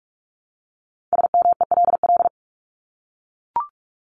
Morse code, heard as a keyed tone, about 44 wpm, 700 Hz. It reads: SMELL
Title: Top 1501-1750 Words 45wpm